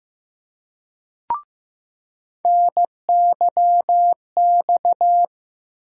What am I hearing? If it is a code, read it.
Morse code, heard as a keyed tone, about 15 wpm, 700 Hz. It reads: NYX
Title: USA Call Sign Postfixes - 3 Letters 15wpm